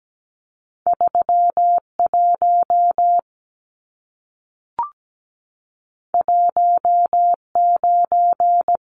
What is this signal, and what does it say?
Morse code, heard as a keyed tone, about 17 wpm, 700 Hz. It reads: 31 19
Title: Numbers - 2-Digit 17wpm